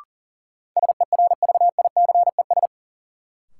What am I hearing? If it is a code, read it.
Morse code, heard as a keyed tone, about 37 wpm, 700 Hz. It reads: SERVICES
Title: Sets of 5 Words - New Word Drill - 40wpm